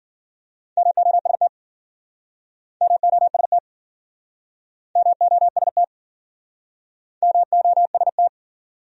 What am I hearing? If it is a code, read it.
Morse code, heard as a keyed tone, about 52 wpm, 700 Hz. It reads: MOST MOST MOST MOST